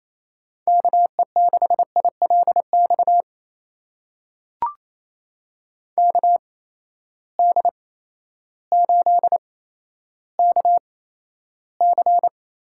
Morse code, heard as a keyed tone, about 28 wpm, spaced as for 13 wpm, 700 Hz. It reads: KE6ILX KD8KC
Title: USA Call Signs - 6 Characters - 8x Character Spacing - 28wpm